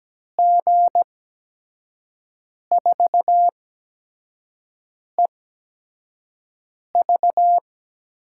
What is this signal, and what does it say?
Morse code, heard as a keyed tone, about 17 wpm, spaced as for 5 wpm, 700 Hz. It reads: G4EV
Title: USA Call Signs - 6 Characters - 8x Character Spacing - 17wpm